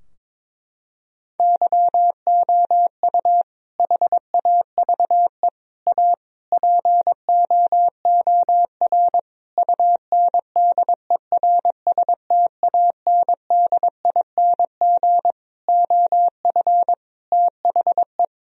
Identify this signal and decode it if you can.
Morse code, heard as a keyed tone, about 22 wpm, 700 Hz. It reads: YOU HAVE A POOR UNDERSTANDING OF THE
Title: Sentences from Top 1000 Words - Review Part 3 - 22wpm